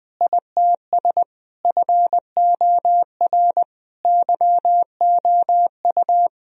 Morse code, heard as a keyed tone, about 20 wpm, 700 Hz. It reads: ITS FOR YOU
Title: Sentences from Top 1000 Words - Review Part 3 - 20wpm